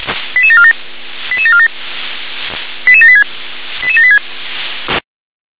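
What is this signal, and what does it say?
Five-tone selective call (ZVEI), heard as quick runs of five tones, tones of about 70 ms. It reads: 88747 88747 88786 88786
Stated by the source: ZVEI five-tone selective call 88058, received by SDR.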